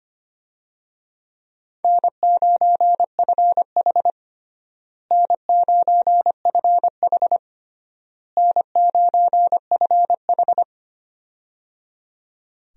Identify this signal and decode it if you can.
Morse code, heard as a keyed tone, about 25 wpm, 700 Hz. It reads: N9FH N9FH N9FH
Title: USA Call Signs - 4 Characters - Repeated 3x - 25wpm